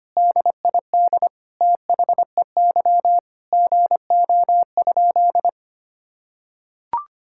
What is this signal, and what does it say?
Morse code, heard as a keyed tone, about 25 wpm, 700 Hz. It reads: DID THEY GO?
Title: Sentences from Top 1000 Words - Review Part 5 - 25wpm